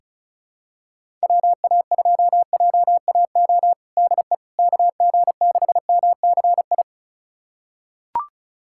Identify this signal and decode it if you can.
Morse code, heard as a keyed tone, about 36 wpm, 700 Hz. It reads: WA2JAO DE KG6MCI
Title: USA Callsign DE Callsign 35wpm